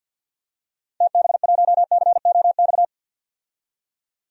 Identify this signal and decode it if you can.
Morse code, heard as a keyed tone, about 51 wpm, 700 Hz. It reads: TB1COX
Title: DXCC-100 Call Signs - 6 Characters - 50wpm